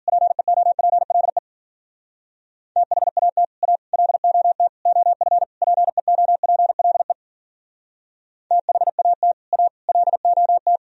Morse code, heard as a keyed tone, about 52 wpm, 700 Hz. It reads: PEOPLE THAT A LOT OF PEOPLE THAT A LOT